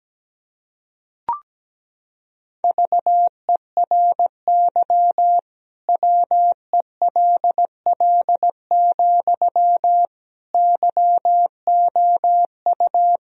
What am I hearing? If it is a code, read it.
Morse code, heard as a keyed tone, about 17 wpm, 700 Hz. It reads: VERY WELL, YOU
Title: Sentences from Top 1000 Words - Review Part 4 - 17wpm